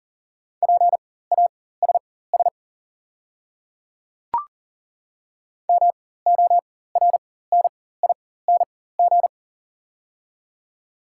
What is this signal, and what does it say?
Morse code, heard as a keyed tone, about 40 wpm, spaced as for 19 wpm, 690 Hz. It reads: PASS MORNING
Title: 4x Character Spacing - Top 500 Words - 40wpm